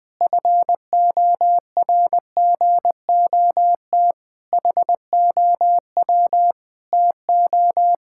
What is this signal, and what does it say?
Morse code, heard as a keyed tone, about 20 wpm, 700 Hz. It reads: FORGOT HOW TO
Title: Pangram Challenge v2 - 20wpm